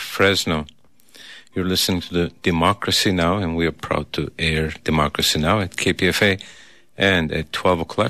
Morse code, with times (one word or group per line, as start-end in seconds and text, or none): none